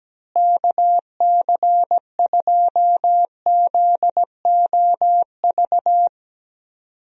Morse code, heard as a keyed tone, about 17 wpm, 700 Hz. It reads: KC2ZOV
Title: USA Call Signs - 6 Characters - Repeated 2x - 17wpm